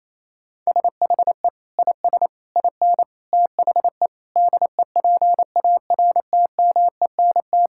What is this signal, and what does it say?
Morse code, heard as a keyed tone, about 28 wpm, 700 Hz. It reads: SHE IS IN THE DEPARTMENT
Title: Sentences from Top 1000 Words - Review Part 5 - 28wpm